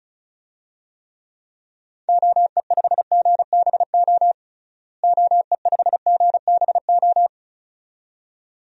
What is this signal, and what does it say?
Morse code, heard as a keyed tone, about 36 wpm, 700 Hz. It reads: OE5GBO OE5GBO
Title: DXCC-100 Call Signs - 6 Characters - Repeated 2x - 35wpm